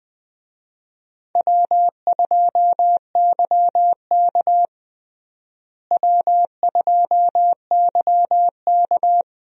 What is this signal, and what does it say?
Morse code, heard as a keyed tone, about 20 wpm, 700 Hz. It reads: W2YK W2YK
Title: USA Call Signs - 4 Characters - Repeated 2x - 20wpm